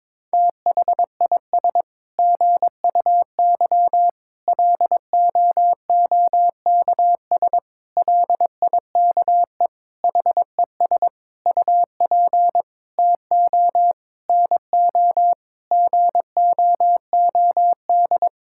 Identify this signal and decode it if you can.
Morse code, heard as a keyed tone, about 22 wpm, 700 Hz. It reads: THIS GUY LOOKS LIKE HES UP TO NO GOOD
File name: Sets of 9 Words - 22wpm